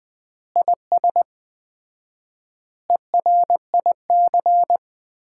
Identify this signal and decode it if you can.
Morse code, heard as a keyed tone, about 21 wpm, 700 Hz. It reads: IS ERIC
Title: Common QSO Phrases - 4x Word-Spacing - 20wpm